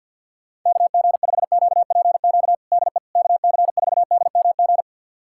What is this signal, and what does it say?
Morse code, heard as a keyed tone, about 46 wpm, 700 Hz. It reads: KG5QPQ DE KC4DMG
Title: USA Callsign DE Callsign - 6 Characters - 50wpm